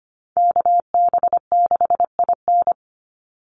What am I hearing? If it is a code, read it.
Morse code, heard as a keyed tone, about 25 wpm, 700 Hz. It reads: KB6IN
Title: USA Call Signs 25wpm